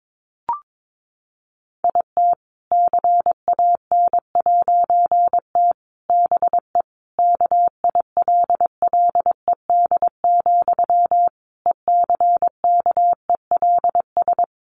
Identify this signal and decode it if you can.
Morse code, heard as a keyed tone, about 22 wpm, 700 Hz. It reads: IT CAN'T BE KILLED, ECKELS
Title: A Sound Of Thunder 22wpm